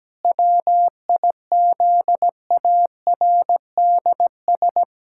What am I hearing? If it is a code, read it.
Morse code, heard as a keyed tone, about 17 wpm, 700 Hz. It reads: WIZARDS
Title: Pangram Challenge v2 - 17wpm